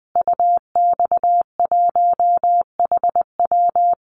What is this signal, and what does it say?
Morse code, heard as a keyed tone, about 19 wpm, 700 Hz. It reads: UX1HW
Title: Call Signs Repeated 2x - 20wpm